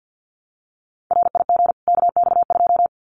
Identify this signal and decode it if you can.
Morse code, heard as a keyed tone, about 50 wpm, 700 Hz. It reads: RIG KX2
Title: Common QSO Phrases 50wpm